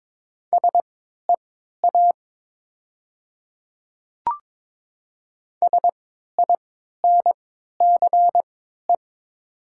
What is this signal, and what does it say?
Morse code, heard as a keyed tone, about 22 wpm, spaced as for 12 wpm, 700 Hz. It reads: SEA SINCE
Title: 3x Character Spacing - Top 300 Words - 22wpm